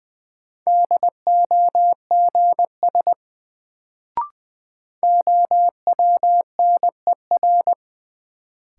Morse code, heard as a keyed tone, about 20 wpm, 700 Hz. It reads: DOGS OWNER
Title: Sentences from Top 1600 Words - New Word Drill Alt - 20wpm